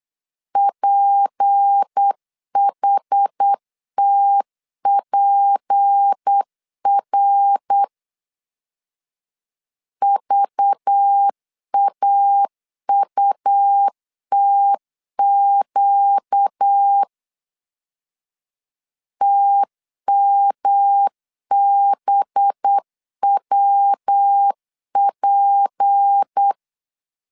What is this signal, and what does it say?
Morse code, heard as a keyed tone, about 8 wpm, 800 Hz. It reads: PHTPR VAUTQ TMBWP
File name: LEZIONI / 6 PKX0 / 6 P 8wpm 3025